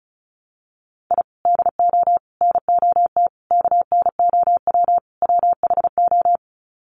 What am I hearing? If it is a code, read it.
Morse code, heard as a keyed tone, about 34 wpm, 700 Hz. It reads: I DO NOT KNOW WHO
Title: Sets of 5 Words 35wpm